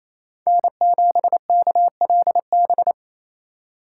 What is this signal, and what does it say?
Morse code, heard as a keyed tone, about 36 wpm, 700 Hz. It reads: N7KLB